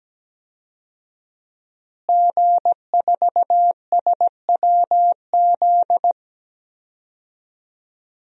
Morse code, heard as a keyed tone, about 17 wpm, 700 Hz. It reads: G4SWZ
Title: DXCC-100 Call Signs - 5 Characters - Repeated 3x - 17wpm